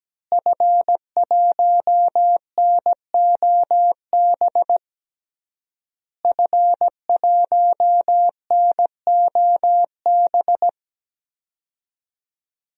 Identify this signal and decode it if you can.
Morse code, heard as a keyed tone, about 17 wpm, 700 Hz. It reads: F1NOB F1NOB
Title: DXCC-100 Call Signs - 5 Characters - Repeated 3x - 17wpm